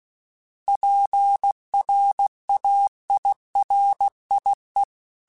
Morse code, heard as a keyed tone, about 16 wpm, 800 Hz. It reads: PRAIRIE
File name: fromSMs / FromSMs mp3s / Cities@15